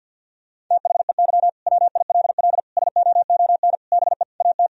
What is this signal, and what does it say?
Morse code, heard as a keyed tone, about 50 wpm, 700 Hz. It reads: THEY WILL SOON BE AT
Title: Sentences from Top 1000 Words - Review Part 4 - 50wpm